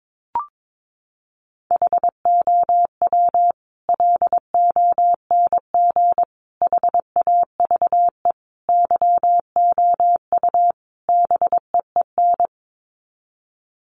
Morse code, heard as a keyed tone, about 22 wpm, 700 Hz. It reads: HOW LONG HAVE YOU BEEN
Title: Sets of 5 Words Encore 22wpm